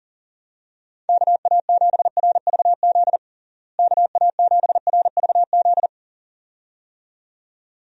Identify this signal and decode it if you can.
Morse code, heard as a keyed tone, about 40 wpm, 700 Hz. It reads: KA7RVZ KA7RVZ
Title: USA Call Signs - 6 Characters - Repeated 2x - 40wpm